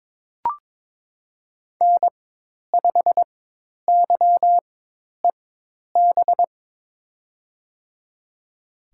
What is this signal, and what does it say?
Morse code, heard as a keyed tone, about 22 wpm, spaced as for 10 wpm, 700 Hz. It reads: N5YEB
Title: USA Call Signs - 4x Character Spacing - 22wpm